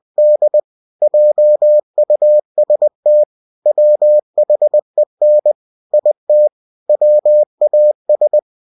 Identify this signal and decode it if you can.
Morse code, heard as a keyed tone, about 20 wpm, 600 Hz. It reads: D JUST WHEN IT WAS